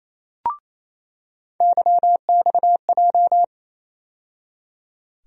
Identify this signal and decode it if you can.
Morse code, heard as a keyed tone, about 28 wpm, 700 Hz. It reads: YXJ
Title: USA Call Sign Postfixes - 3 Letters 28wpm